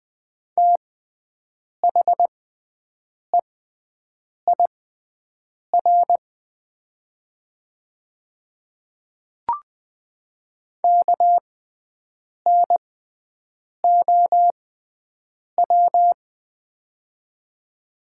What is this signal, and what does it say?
Morse code, heard as a keyed tone, about 20 wpm, spaced as for 7 wpm, 700 Hz. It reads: THEIR KNOW
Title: 6x Character Spacing - Top 100 Words - 20wpm